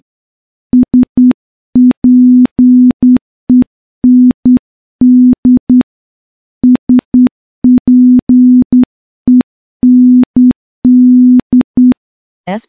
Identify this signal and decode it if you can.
Morse code, heard as a keyed tone, about 10 wpm, 250 Hz. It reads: SPEND SPEND
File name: morse-training-10WPM